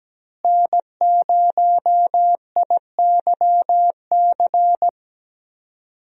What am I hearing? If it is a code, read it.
Morse code, heard as a keyed tone, about 17 wpm, 700 Hz. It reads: N0IYC
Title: USA Call Signs - 5 Characters - Repeated 3x - 17wpm